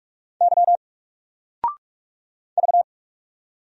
Morse code, heard as a keyed tone, about 46 wpm, 700 Hz. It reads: Y V
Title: Single Letter-Number 45wpm